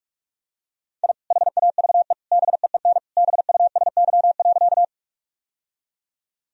Morse code, heard as a keyed tone, about 46 wpm, 700 Hz. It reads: I HAVE BEEN BUSY.
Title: Sentences from Top 1000 Words - Review Part 3 - 45wpm